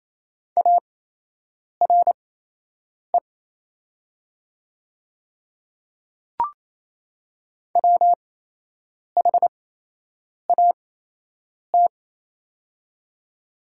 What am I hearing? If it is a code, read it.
Morse code, heard as a keyed tone, about 27 wpm, spaced as for 8 wpm, 700 Hz. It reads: ARE WHAT
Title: 8x Character Spacing - Top 200 Words - 28wpm